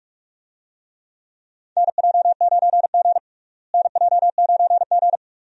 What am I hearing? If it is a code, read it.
Morse code, heard as a keyed tone, about 46 wpm, 700 Hz. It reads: NJ9G NJ9G
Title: USA Call Signs - 4 Characters - Repeated 2x - 45wpm